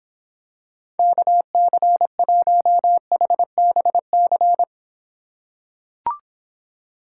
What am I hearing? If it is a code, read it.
Morse code, heard as a keyed tone, about 25 wpm, 700 Hz. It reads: KC1HBC